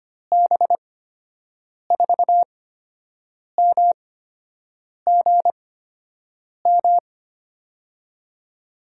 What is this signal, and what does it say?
Morse code, heard as a keyed tone, about 25 wpm, 700 Hz. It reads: B 4 M G M
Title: USA Call Signs - 8x Character Spacing - 25wpm